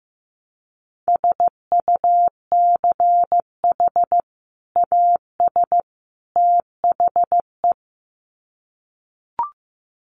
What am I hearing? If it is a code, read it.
Morse code, heard as a keyed tone, about 15 wpm, 700 Hz. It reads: SUCH AS THE
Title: Sets of 3 Words 15wpm